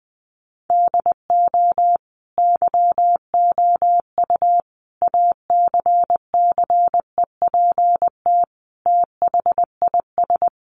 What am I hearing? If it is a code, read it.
Morse code, heard as a keyed tone, about 20 wpm, 700 Hz. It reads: DO YOU ACCEPT THIS
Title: Sentences from Top 1300 Words 20wpm